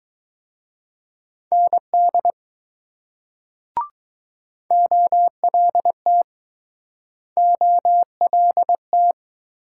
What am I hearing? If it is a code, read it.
Morse code, heard as a keyed tone, about 20 wpm, 700 Hz. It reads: ND OLT OLT